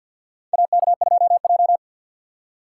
Morse code, heard as a keyed tone, about 52 wpm, 700 Hz. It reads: AK2J
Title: USA Call Signs - 2x Character Spacing - 50wpm